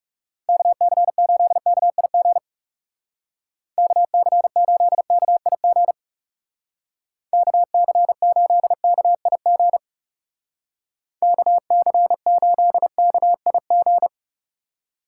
Morse code, heard as a keyed tone, about 36 wpm, 700 Hz. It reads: KC8KIG KC8KIG KC8KIG KC8KIG
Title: Speed Racing - USA 2x3 Call Signs 30wpm